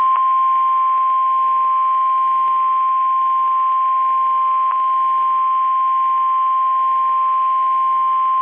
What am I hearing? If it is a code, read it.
Emergency Alert System (SAME) tones (the 1050 Hz attention tone). It is heard from 0.9 to 7.6 s.
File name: eas messages / EAS WXM31 Elmira